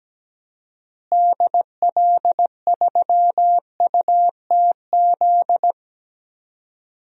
Morse code, heard as a keyed tone, about 17 wpm, 700 Hz. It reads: DL3UTZ
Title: DXCC-100 Call Signs - 6 Characters - Repeated 3x - 17wpm